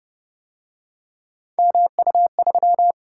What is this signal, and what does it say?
Morse code, heard as a keyed tone, about 29 wpm, 700 Hz. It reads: MU3